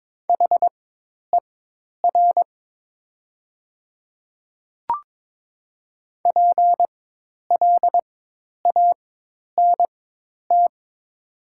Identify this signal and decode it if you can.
Morse code, heard as a keyed tone, about 22 wpm, spaced as for 10 wpm, 700 Hz. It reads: HER PLANT
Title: 4x Character Spacing - Top 500 Words - 22wpm